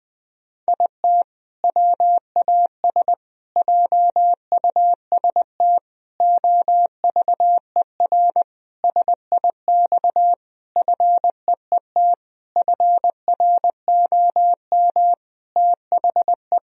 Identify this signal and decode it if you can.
Morse code, heard as a keyed tone, about 20 wpm, 700 Hz. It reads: IT WAS JUST OVER SIX FEET FROM THE
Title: Sentences from Top 1000 Words - Review Part 2 - 20wpm